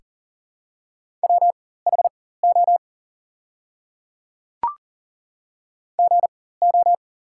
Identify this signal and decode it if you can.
Morse code, heard as a keyed tone, about 40 wpm, spaced as for 19 wpm, 700 Hz. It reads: WHO GO
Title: 4x Character Spacing - Top 100 Words - 40wpm